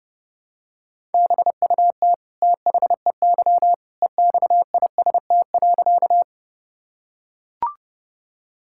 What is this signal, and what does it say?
Morse code, heard as a keyed tone, about 32 wpm, 700 Hz. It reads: BUT THEY EXIST.
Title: Sentences from Top 1000 Words - Review Part 5 - 30wpm